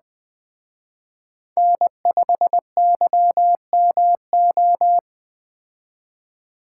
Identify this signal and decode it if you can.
Morse code, heard as a keyed tone, about 20 wpm, 700 Hz. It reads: N5YMO
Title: Speed Racing - USA 1x3 Call Signs 17wpm